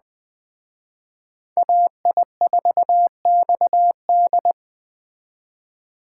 Morse code, heard as a keyed tone, about 20 wpm, 700 Hz. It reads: AI4XD
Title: Speed Racing - USA Call Signs 17wpm